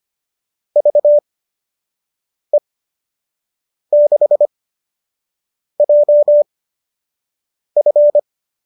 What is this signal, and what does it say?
Morse code, heard as a keyed tone, about 25 wpm, spaced as for 6 wpm, 590 Hz. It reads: VE6JF
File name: qso002 06